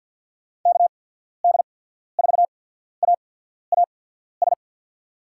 Morse code, heard as a keyed tone, about 50 wpm, spaced as for 14 wpm, 700 Hz. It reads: KD4AAS
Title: USA Call Signs - 6 Characters - 8x Character Spacing - 50wpm